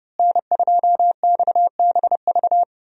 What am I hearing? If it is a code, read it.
Morse code, heard as a keyed tone, about 32 wpm, 700 Hz. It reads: N2XBV
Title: USA Call Signs - 5 Characters - Repeated 3x - 30wpm